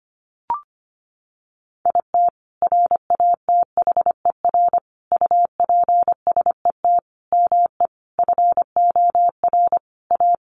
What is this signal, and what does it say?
Morse code, heard as a keyed tone, about 25 wpm, 700 Hz. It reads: IT RATHER UPSET ME FOR A
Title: Sentences from Top 900 Words 25wpm